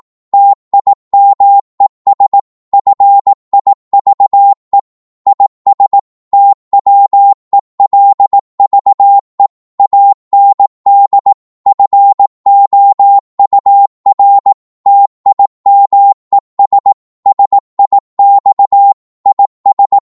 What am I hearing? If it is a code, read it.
Morse code, heard as a keyed tone, about 18 wpm, 800 Hz. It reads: TIMES FIVE IS TWELVE AND FOUR TIMES SIX IS